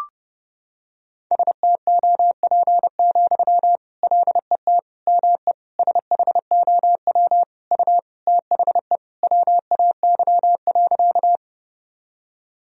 Morse code, heard as a keyed tone, about 30 wpm, 700 Hz. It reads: STOP, LET ME SHOW U THE WAY.
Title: Sentences from Top 300 Words and Abbreviations 30wpm